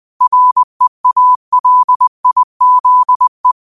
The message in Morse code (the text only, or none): REALIZE